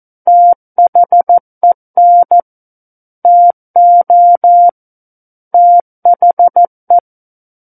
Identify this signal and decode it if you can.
Morse code, heard as a keyed tone, about 14 wpm, 700 Hz. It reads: THEN TO THE
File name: Book One / 05-Chapter 5 - 14 WPM